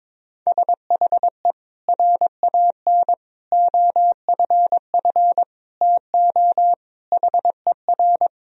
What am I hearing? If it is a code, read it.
Morse code, heard as a keyed tone, about 21 wpm, 700 Hz. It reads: SHE RAN OFF TO HER
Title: Sentences from Top 1000 Words - Review Part 3 - 22wpm